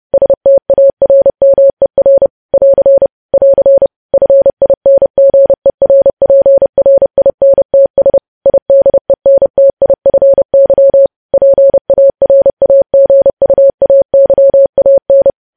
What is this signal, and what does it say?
Morse code, heard as a keyed tone, about 30 wpm, 550 Hz. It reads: STARMER <AR> <AR> FINGERPRINTS IDENTIFY PARAGUAYAN
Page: news headlines in morse code at 30 wpm